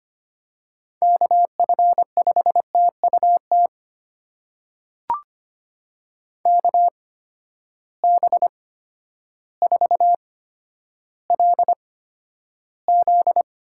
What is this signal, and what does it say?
Morse code, heard as a keyed tone, about 25 wpm, spaced as for 11 wpm, 700 Hz. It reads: KF5TUT KB4LZ
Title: USA Call Signs - 6 Characters - 8x Character Spacing - 25wpm